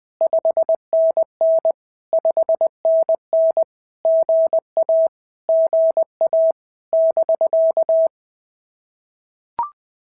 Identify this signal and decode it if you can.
Morse code, heard as a keyed tone, about 20 wpm, 650 Hz. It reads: 5NN 5NN GA GA <BK>
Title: POTA QSO Practice - 20wpm